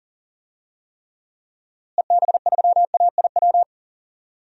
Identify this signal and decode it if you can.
Morse code, heard as a keyed tone, about 40 wpm, 700 Hz. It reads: EB3AIW
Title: DXCC-100 Call Signs - 6 Characters - Repeated 3x - 40wpm